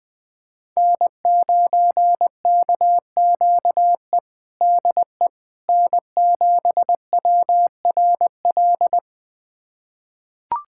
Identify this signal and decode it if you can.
Morse code, heard as a keyed tone, about 20 wpm, 700 Hz. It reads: N9KQE DE N7WRL
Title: USA Callsign DE Callsign - 5 Characters - 20wpm